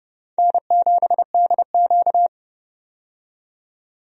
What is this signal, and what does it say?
Morse code, heard as a keyed tone, about 29 wpm, 700 Hz. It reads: N7DQ
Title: USA Call Signs - 4 Characters - 30wpm